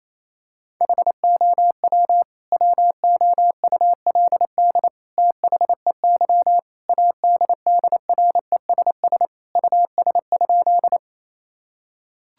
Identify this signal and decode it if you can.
Morse code, heard as a keyed tone, about 28 wpm, 700 Hz. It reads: HOW WOULD THEY ADDRESS US?
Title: Sentences from Top 1000 Words - Review Part 6 - 28wpm